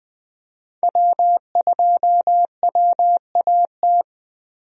Morse code, heard as a keyed tone, about 19 wpm, 700 Hz. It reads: W2WAT
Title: USA Call Signs - 5 Characters - Repeated 3x - 20wpm